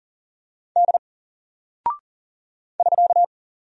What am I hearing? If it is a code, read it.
Morse code, heard as a keyed tone, about 40 wpm, 700 Hz. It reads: D <SK>
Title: Single Character (Letter:Number:Punctuation:Prosign) 40wpm